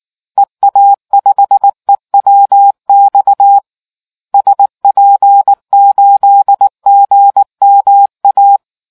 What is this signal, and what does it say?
Morse code, heard as a keyed tone, about 19 wpm, 800 Hz. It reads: EA5EWX SP8GMA